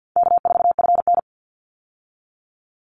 Morse code, heard as a keyed tone, about 50 wpm, 700 Hz. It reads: K4FN
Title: Call Signs Repeated 2x - 50wpm